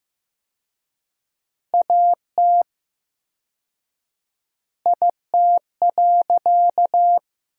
Morse code, heard as a keyed tone, about 15 wpm, 700 Hz. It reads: AT IT.
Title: 4x Word-Spacing - Sentences from Top 300 Words - 15wpm